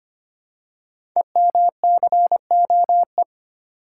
Morse code, heard as a keyed tone, about 25 wpm, 700 Hz. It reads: EMCOE
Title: Code Groups - 5 Letter:Numbers - 25wpm